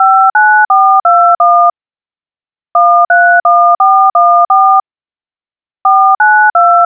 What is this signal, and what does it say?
DTMF, heard as telephone touch tones, tones of about 330 ms. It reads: 59421 131414 492